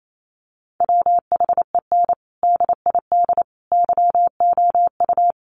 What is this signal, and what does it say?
Morse code, heard as a keyed tone, about 28 wpm, 700 Hz. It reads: WHEN DID YOU
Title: Sentences from Top 300 Words 28wpm